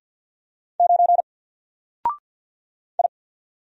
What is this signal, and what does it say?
Morse code, heard as a keyed tone, about 48 wpm, 700 Hz. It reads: 9 I
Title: Single Letter-Number Alt 50wpm